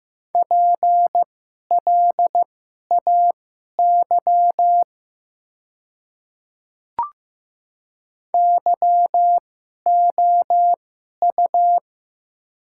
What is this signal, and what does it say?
Morse code, heard as a keyed tone, about 15 wpm, spaced as for 11 wpm, 700 Hz. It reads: PLAY YOU
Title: 2x Character Spacing - Top 200 Words - 15wpm